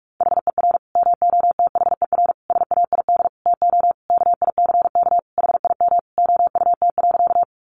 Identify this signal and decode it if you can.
Morse code, heard as a keyed tone, about 44 wpm, 700 Hz. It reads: HER MOTHER SAID TO KICK HIM OUT.
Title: Sentences from Top 1300 Words 45wpm